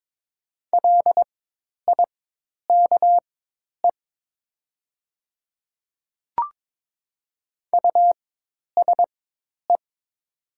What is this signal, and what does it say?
Morse code, heard as a keyed tone, about 22 wpm, spaced as for 10 wpm, 700 Hz. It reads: LIKE USE
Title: 4x Character Spacing - Top 200 Words - 22wpm